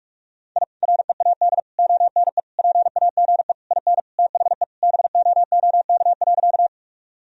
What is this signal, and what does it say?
Morse code, heard as a keyed tone, about 42 wpm, 700 Hz. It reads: I READ ONE PAGE IN THE BOOK.
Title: Sentences from Top 1000 Words - Review Part 1 - 45wpm